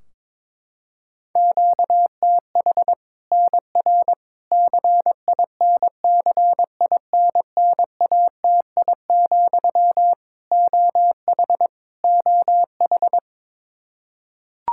Morse code, heard as a keyed tone, about 21 wpm, 700 Hz. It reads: QTH NR CINCINNATI, OH OH
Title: Practice QSO - Exchange 1 - Beginning - 22wpm